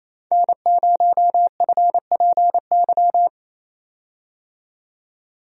Morse code, heard as a keyed tone, about 27 wpm, 700 Hz. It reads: N0FPY
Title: USA Call Signs - 5 Characters - 28wpm